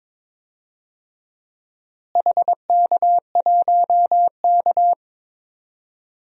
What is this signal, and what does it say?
Morse code, heard as a keyed tone, about 21 wpm, 700 Hz. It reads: HK1K
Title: DXCC-100 Call Signs - 4 Characters - Repeated 3x - 22wpm